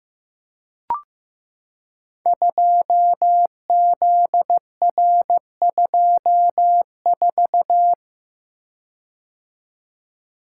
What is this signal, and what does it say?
Morse code, heard as a keyed tone, about 15 wpm, 700 Hz. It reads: 2ZR24
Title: Code Groups - 5 Letter:Numbers - 15wpm